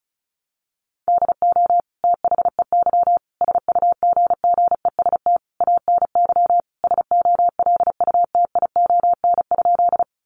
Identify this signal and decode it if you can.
Morse code, heard as a keyed tone, about 34 wpm, 700 Hz. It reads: DO THEY SUGGEST ANY SOLUTION?
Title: Sentences from Top 1300 Words 35wpm